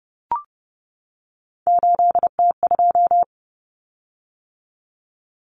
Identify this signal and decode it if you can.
Morse code, heard as a keyed tone, about 30 wpm, 700 Hz. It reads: 8T2
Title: Common Cut Numbers - 3-Digit 30wpm